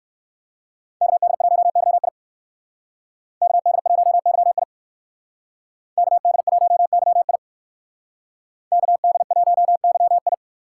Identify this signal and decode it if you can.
Morse code, heard as a keyed tone, about 56 wpm, 700 Hz. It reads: KD1YI KD1YI KD1YI KD1YI
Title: Speed Racing - USA Call Signs 45wpm